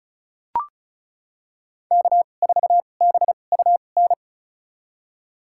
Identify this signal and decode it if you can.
Morse code, heard as a keyed tone, about 36 wpm, spaced as for 25 wpm, 700 Hz. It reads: K4BUN
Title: USA Call Signs - 5 Characters - 2x Character Spacing - 35wpm